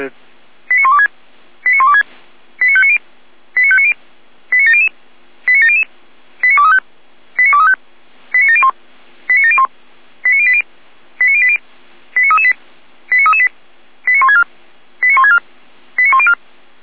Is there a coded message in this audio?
Yes: five-tone selective call (ZVEI) from 0.7 to 16.3 s, heard as quick runs of five tones, tones of about 70 ms, with speech. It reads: 89126 89126 89590 89590 89700 89700 89235 89235 89701 89701 89080 89080 89338 89338 89164 89164 89104